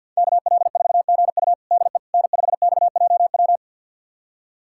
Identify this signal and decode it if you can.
Morse code, heard as a keyed tone, about 50 wpm, 700 Hz. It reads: KL4GU DE N5XJW
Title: USA Callsign DE Callsign - 5 Characters - 50wpm